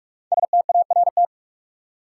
Morse code, heard as a keyed tone, about 44 wpm, 700 Hz. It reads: START
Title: Sets of 4 Words - New Word Drill - 45wpm